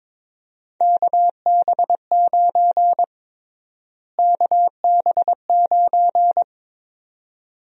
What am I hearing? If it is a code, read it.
Morse code, heard as a keyed tone, about 22 wpm, 700 Hz. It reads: KB9 KB9
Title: USA Call Sign Prefixes - Repeated 2x - 22wpm